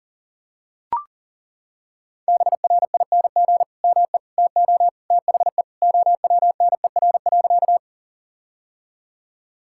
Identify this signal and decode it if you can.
Morse code, heard as a keyed tone, about 40 wpm, 700 Hz. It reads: BRING ME TO THE OWNER.
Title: Sentences from Top 1600 Words 40wpm